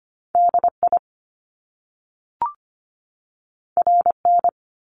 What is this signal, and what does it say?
Morse code, heard as a keyed tone, about 25 wpm, 700 Hz. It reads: DI RN
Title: Top Two Letter Combinations in Words 25wpm